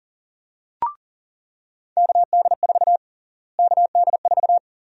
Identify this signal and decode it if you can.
Morse code, heard as a keyed tone, about 40 wpm, 700 Hz. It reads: KD4 KD4
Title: USA Call Sign Prefixes - Repeated 2x - 40wpm